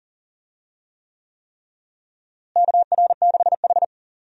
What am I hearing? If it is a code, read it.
Morse code, heard as a keyed tone, about 40 wpm, 700 Hz. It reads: KR6H
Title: USA Call Signs - 4 Characters - Repeated 3x - 40wpm